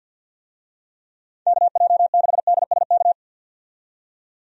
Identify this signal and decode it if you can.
Morse code, heard as a keyed tone, about 46 wpm, 700 Hz. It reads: KJ6DSK
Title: USA Call Signs - 6 Characters - 50wpm